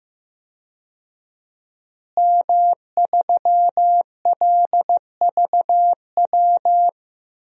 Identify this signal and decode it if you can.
Morse code, heard as a keyed tone, about 15 wpm, 700 Hz. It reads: M3LVW
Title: DXCC-100 Call Signs - 5 Characters - Repeated 3x - 15wpm